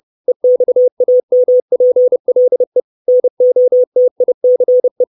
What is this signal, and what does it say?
Morse code, heard as a keyed tone, about 30 wpm, 500 Hz. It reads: EXAMPLE NOTICE
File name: 632 words / 632 Words 30WPM